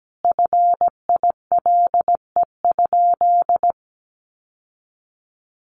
Morse code, heard as a keyed tone, about 17 wpm, 700 Hz. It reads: FILE?
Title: Sentences from Top 1200 Words 17wpm